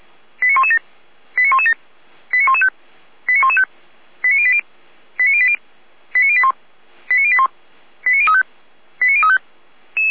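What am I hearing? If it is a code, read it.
Five-tone selective call (ZVEI), heard as quick runs of five tones, tones of about 70 ms. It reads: 89117 89117 89115 89115 89080 89080 89081 89081 89035 89035
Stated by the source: ZVEI five-tone selective call received by SDR, on the server of the Orlamünde fire brigade (Feuerwehr Orlamünde).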